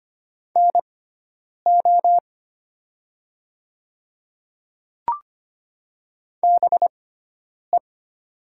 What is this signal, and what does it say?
Morse code, heard as a keyed tone, about 25 wpm, spaced as for 9 wpm, 700 Hz. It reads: NO BE
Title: 6x Character Spacing - Top 200 Words - 25wpm